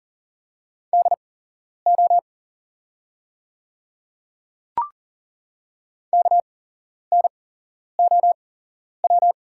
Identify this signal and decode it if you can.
Morse code, heard as a keyed tone, about 41 wpm, spaced as for 11 wpm, 700 Hz. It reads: DO KNOW
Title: 8x Character Spacing - Top 100 Words - 40wpm